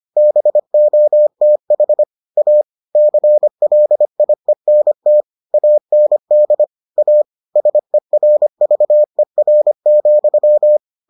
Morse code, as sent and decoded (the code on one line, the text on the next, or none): -... --- - .... / .- / -.-. .-.. .. -. - / .- -. -.. / .- / ... .-. ...- . .-. --..--
BOTH A CLIENT AND A SERVER,